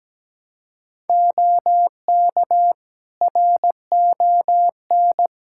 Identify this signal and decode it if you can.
Morse code, heard as a keyed tone, about 17 wpm, 700 Hz. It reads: OK RON
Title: Practice QSO - Exchange 3 - More Info - 17wpm